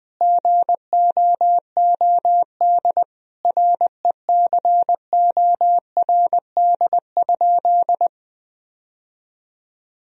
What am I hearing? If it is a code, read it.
Morse code, heard as a keyed tone, about 20 wpm, 700 Hz. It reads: GOOD RECORD?
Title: Sentences from Top 1000 Words - Review Part 3 - 20wpm